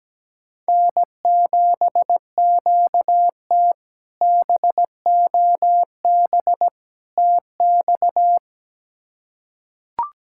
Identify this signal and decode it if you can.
Morse code, heard as a keyed tone, about 17 wpm, 700 Hz. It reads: N7QT BOB TX
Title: K1USN SST QSO Practice - 17wpm